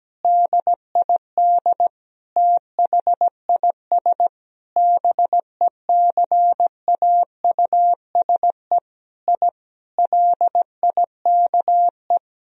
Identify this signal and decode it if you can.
Morse code, heard as a keyed tone, about 17 wpm, 700 Hz. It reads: DID THIS BECAUSE I LIKE
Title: Sentences from Top 1000 Words - Review Part 3 - 17wpm